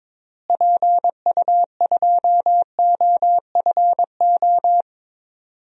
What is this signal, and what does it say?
Morse code, heard as a keyed tone, about 22 wpm, 700 Hz. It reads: PU2OFO
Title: DXCC-100 Call Signs - 6 Characters - Repeated 3x - 22wpm